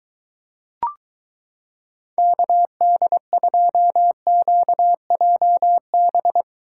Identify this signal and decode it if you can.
Morse code, heard as a keyed tone, about 23 wpm, 700 Hz. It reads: KD2QJB